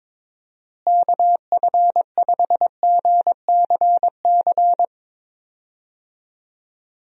Morse code, heard as a keyed tone, about 22 wpm, 700 Hz. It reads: KF5GCC
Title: USA Call Signs - 6 Characters - Repeated 2x - 22wpm